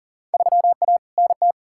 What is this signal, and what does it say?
Morse code, heard as a keyed tone, about 40 wpm, 700 Hz. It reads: 3A NT
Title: Field Day Exchanges Encore - 40wpm